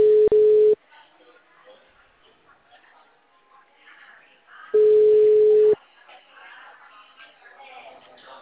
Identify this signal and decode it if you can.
Ringback tone (Europe, 425 Hz), with speech.